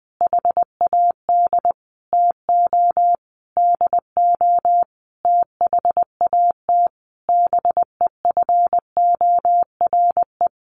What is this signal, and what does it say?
Morse code, heard as a keyed tone, about 20 wpm, 700 Hz. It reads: HAD TO DO THAT BEFORE
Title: Sentences from Top 1300 Words 20wpm